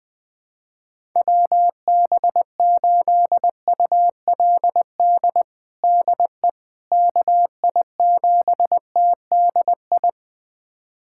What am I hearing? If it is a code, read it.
Morse code, heard as a keyed tone, about 20 wpm, 700 Hz. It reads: WB8ULD DE KI7TDI
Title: USA Callsign DE Callsign - 6 Characters - 20wpm